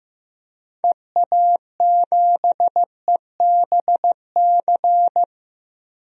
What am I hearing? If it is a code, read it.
Morse code, heard as a keyed tone, about 15 wpm, 700 Hz. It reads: EA7EBC
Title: DXCC-100 Call Signs - 6 Characters - 15wpm